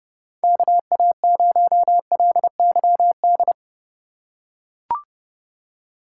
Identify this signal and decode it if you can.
Morse code, heard as a keyed tone, about 30 wpm, 700 Hz. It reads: KA0LYD
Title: USA Call Signs - 2x Character Spacing - 30wpm